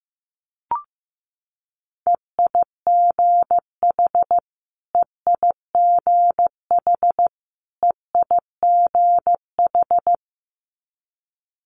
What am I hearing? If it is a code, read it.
Morse code, heard as a keyed tone, about 15 wpm, 700 Hz. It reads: EIGH EIGH EIGH
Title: Multi-letter Phonemes Repeated 3x 15wpm